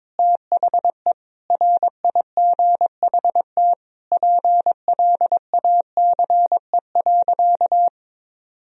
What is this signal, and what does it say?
Morse code, heard as a keyed tone, about 21 wpm, 700 Hz. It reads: THE RIGHT PLACE.
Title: Sentences from Top 1000 Words - Review Part 3 - 22wpm